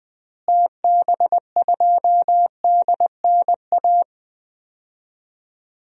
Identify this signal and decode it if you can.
Morse code, heard as a keyed tone, about 20 wpm, 700 Hz. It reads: TB2DNA
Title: DXCC-100 Call Signs - 6 Characters - Repeated 2x - 20wpm